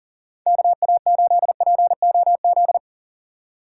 Morse code, heard as a keyed tone, about 40 wpm, 700 Hz. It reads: KA8POZ
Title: K1USN SST QSO Practice - 40wpm